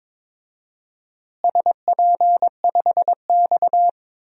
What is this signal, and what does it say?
Morse code, heard as a keyed tone, about 22 wpm, 700 Hz. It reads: SP5X
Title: DXCC-100 Call Signs - 4 Characters - Repeated 2x - 22wpm